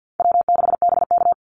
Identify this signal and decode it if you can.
Morse code, heard as a keyed tone, about 50 wpm, 700 Hz. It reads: W6BK
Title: Easy Call Signs Repeated 3x - 50wpm